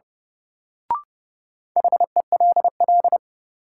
Morse code, heard as a keyed tone, about 30 wpm, 700 Hz. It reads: HELL
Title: Sets of 5 Words Encore - New Word Drill - 30wpm